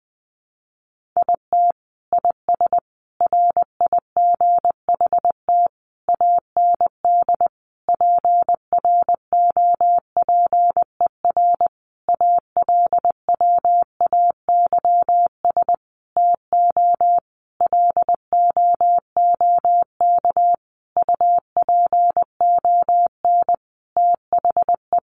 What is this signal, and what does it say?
Morse code, heard as a keyed tone, about 20 wpm, 700 Hz. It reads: IT IS RIGHT AND PROPER ALWAYS TO LOOK UPON THE
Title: Sentences from Top 1500 Words 20wpm